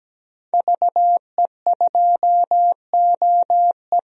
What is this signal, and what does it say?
Morse code, heard as a keyed tone, about 17 wpm, 700 Hz. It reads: VE2OE
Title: DXCC-100 Call Signs - 5 Characters - Repeated 2x - 17wpm